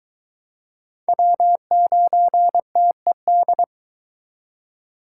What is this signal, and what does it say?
Morse code, heard as a keyed tone, about 22 wpm, 700 Hz. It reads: W9TED